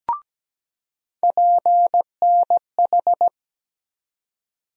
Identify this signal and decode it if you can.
Morse code, heard as a keyed tone, about 17 wpm, 700 Hz. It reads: PNH
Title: USA Call Sign Postfixes - 3 Letters 17wpm